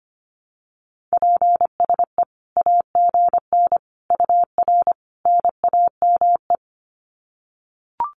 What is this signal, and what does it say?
Morse code, heard as a keyed tone, about 25 wpm, 700 Hz. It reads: PSE AGN UR NAME
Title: Common QSO Phrases 25wpm